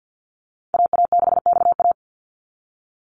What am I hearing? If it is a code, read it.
Morse code, heard as a keyed tone, about 50 wpm, 700 Hz. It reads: AA6XA
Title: NA SOTA Activator Callsigns 50wpm